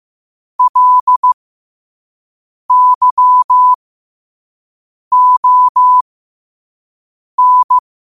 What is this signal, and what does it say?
Morse code, heard as a keyed tone, about 15 wpm, spaced as for 5 wpm, 1000 Hz. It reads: LYON
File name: qotdfiles / 5wpm